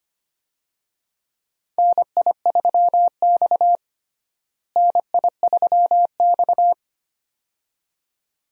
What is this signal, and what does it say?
Morse code, heard as a keyed tone, about 25 wpm, 700 Hz. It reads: NI3X NI3X
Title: USA Call Signs - 4 Characters - Repeated 2x - 25wpm